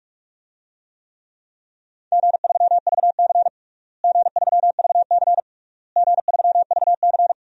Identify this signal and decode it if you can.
Morse code, heard as a keyed tone, about 42 wpm, 700 Hz. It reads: G3VC G3VC G3VC
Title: DXCC-100 Call Signs - 4 Characters - Repeated 3x - 45wpm